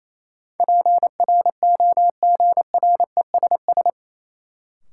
Morse code, heard as a keyed tone, about 28 wpm, 700 Hz. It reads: PROGRESS
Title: Sentences from Top 1400 Words - New Word Drill - 28wpm